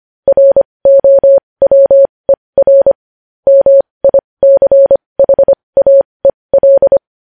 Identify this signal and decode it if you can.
Morse code, heard as a keyed tone, about 25 wpm, spaced as for 20 wpm, 550 Hz. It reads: ROWER MICHAEL